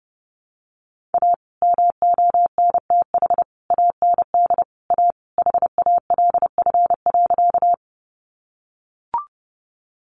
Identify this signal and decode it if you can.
Morse code, heard as a keyed tone, about 30 wpm, 700 Hz. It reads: A MONTH AND A HALF.
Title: Sentences from Top 800 Words 30wpm